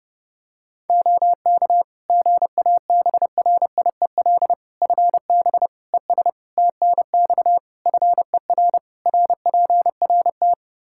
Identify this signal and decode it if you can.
Morse code, heard as a keyed tone, about 30 wpm, 700 Hz. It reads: OK GABRIEL FB ES TNX FER RPRT
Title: Practice QSO - Exchange 2 - Info - 30wpm